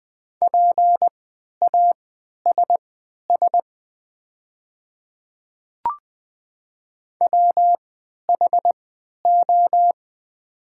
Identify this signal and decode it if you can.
Morse code, heard as a keyed tone, about 20 wpm, spaced as for 11 wpm, 700 Hz. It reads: PASS WHO
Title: 3x Character Spacing - Top 500 Words - 20wpm